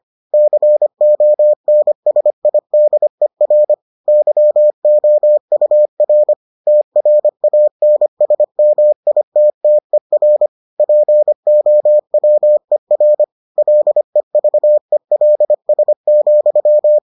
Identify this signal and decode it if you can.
Morse code, heard as a keyed tone, about 25 wpm, 600 Hz. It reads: CONSIDER YOUR TRANSMITTER POWER LEVELS,